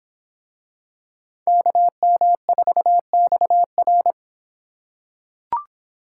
Morse code, heard as a keyed tone, about 26 wpm, 700 Hz. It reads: KM4XR